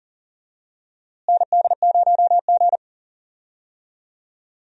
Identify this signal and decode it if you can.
Morse code, heard as a keyed tone, about 40 wpm, 700 Hz. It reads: ND0G
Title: USA Call Signs - 4 Characters - Repeated 2x - 40wpm